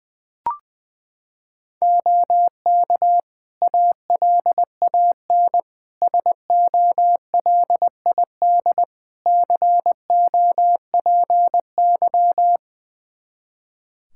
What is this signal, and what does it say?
Morse code, heard as a keyed tone, about 20 wpm, 700 Hz. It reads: OK ALAN SOLID COPY
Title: Practice QSO - Exchange 3 - More Info - 20wpm